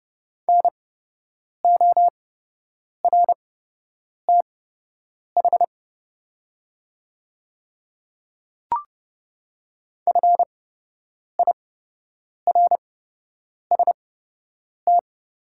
Morse code, heard as a keyed tone, about 30 wpm, spaced as for 8 wpm, 700 Hz. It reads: NORTH FIRST
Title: 8x Character Spacing - Top 300 Words - 30wpm